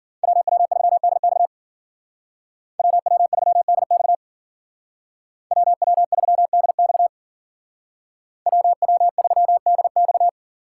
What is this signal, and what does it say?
Morse code, heard as a keyed tone, about 50 wpm, 700 Hz. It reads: WW3DX WW3DX WW3DX WW3DX
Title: Speed Racing - USA 2x2 Call Signs 40wpm